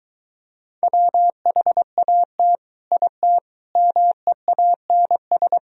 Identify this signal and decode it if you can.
Morse code, heard as a keyed tone, about 23 wpm, 700 Hz. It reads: WHAT IT MEANS